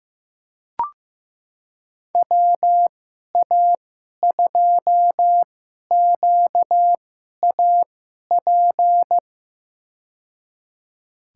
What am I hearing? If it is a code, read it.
Morse code, heard as a keyed tone, about 15 wpm, spaced as for 11 wpm, 700 Hz. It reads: WA2QAP
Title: USA Call Signs - 2x Character Spacing - 15wpm